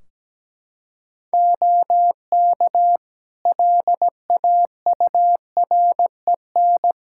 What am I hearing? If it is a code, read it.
Morse code, heard as a keyed tone, about 17 wpm, 700 Hz. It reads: OK LAUREN
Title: Practice QSO - Exchange 4 - Ending - 17wpm